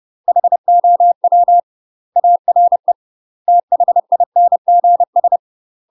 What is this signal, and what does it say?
Morse code, heard as a keyed tone, about 30 wpm, 700 Hz. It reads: HOW ARE THINGS
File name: UK MASTER QSOs TRAINING / 30WPM / UK MASTER QSOs 0018